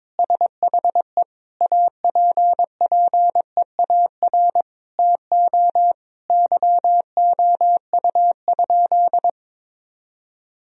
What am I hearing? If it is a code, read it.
Morse code, heard as a keyed tone, about 22 wpm, 700 Hz. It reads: SHE APPEAR TO YOU?
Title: Sentences from Top 1000 Words - Review Part 2 - 22wpm